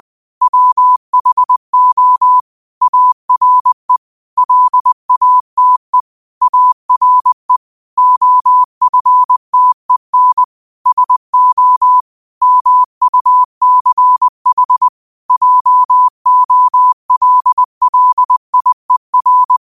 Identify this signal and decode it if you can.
Morse code, heard as a keyed tone, about 20 wpm, 1000 Hz. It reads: WHO ARE LATE ARE OFTEN SO MUCH JOLLIER
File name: qotdfiles / 20wpm